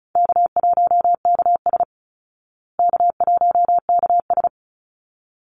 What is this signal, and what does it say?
Morse code, heard as a keyed tone, about 34 wpm, 700 Hz. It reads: K1KS K1KS
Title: Easy Call Signs Repeated 2x - 35wpm